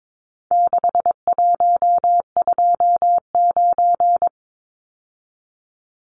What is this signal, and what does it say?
Morse code, heard as a keyed tone, about 22 wpm, 700 Hz. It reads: 6129
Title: Numbers - 4-Digit 22wpm